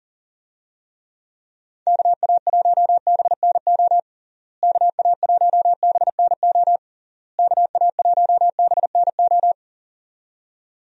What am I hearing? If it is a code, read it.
Morse code, heard as a keyed tone, about 40 wpm, 700 Hz. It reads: KA1BNO KA1BNO KA1BNO
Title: USA Call Signs - 6 Characters - Repeated 3x - 40wpm